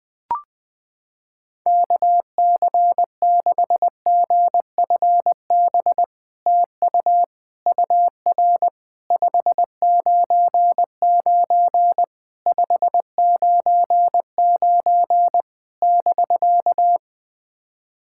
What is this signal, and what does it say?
Morse code, heard as a keyed tone, about 20 wpm, 700 Hz. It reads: KC6GFB TU UR 599 599 <BK>
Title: SOTA QSO Practice 20wpm